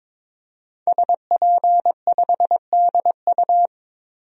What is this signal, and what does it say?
Morse code, heard as a keyed tone, about 21 wpm, 700 Hz. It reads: SP5DU
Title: DXCC-100 Call Signs - 5 Characters - 22wpm